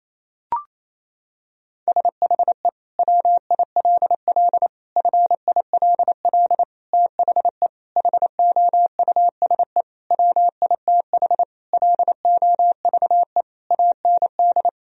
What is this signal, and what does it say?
Morse code, heard as a keyed tone, about 28 wpm, 700 Hz. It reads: SHE WILL FILL THE HOUSE WITH LOVE AND
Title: Sentences from Top 1000 Words - Review Part 3 - 28wpm